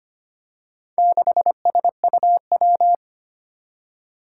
Morse code, heard as a keyed tone, about 25 wpm, 700 Hz. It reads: 6SUW
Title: Four Challenging Letter-Number Combinations 25wpm